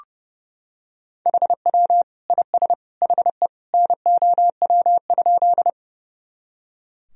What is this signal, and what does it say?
Morse code, heard as a keyed tone, about 30 wpm, 700 Hz. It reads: HW IS HE NOW?
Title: Sentences from Top 100 Words and Abbreviations 30wpm